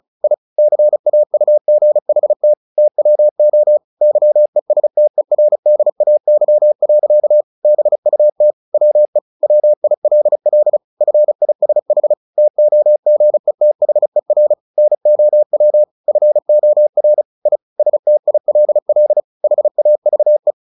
Morse code, heard as a keyed tone, about 34 wpm, 600 Hz. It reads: I CAUGHT TWO YESTERDAY. BUT WE WILL FISH TOGETHER NOW FOR I STILL HAVE
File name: the old man and the sea 2 35wpm